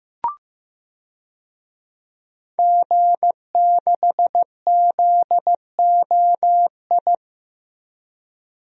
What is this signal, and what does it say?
Morse code, heard as a keyed tone, about 15 wpm, 700 Hz. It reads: G6ZOI
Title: DXCC-100 Call Signs - 5 Characters - 15wpm